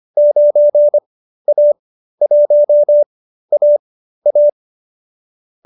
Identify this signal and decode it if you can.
Morse code, heard as a keyed tone, about 25 wpm, spaced as for 13 wpm, 590 Hz. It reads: 9A1AA